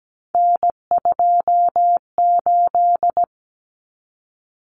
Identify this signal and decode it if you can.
Morse code, heard as a keyed tone, about 17 wpm, 700 Hz. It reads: N28
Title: Common Cut Numbers - 3-Digit 17wpm